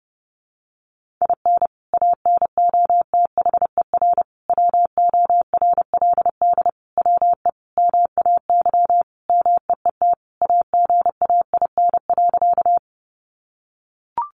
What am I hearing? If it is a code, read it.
Morse code, heard as a keyed tone, about 30 wpm, 700 Hz. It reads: IN ANOTHER WORLD WE MAY MEET AGAIN.
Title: Sentences from Top 1100 Words 30wpm